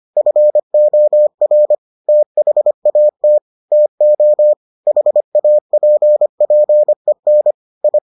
Morse code, heard as a keyed tone, about 25 wpm, 600 Hz. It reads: FOR THAT TO HAPPEN I